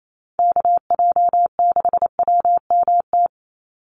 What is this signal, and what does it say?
Morse code, heard as a keyed tone, about 28 wpm, 700 Hz. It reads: KJ6WMT
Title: USA Call Signs 28wpm